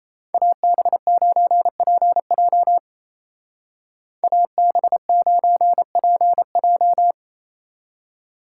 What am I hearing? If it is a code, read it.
Morse code, heard as a keyed tone, about 34 wpm, 700 Hz. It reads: AB9PJ AB9PJ
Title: Speed Racing - USA 2x2 Call Signs 28wpm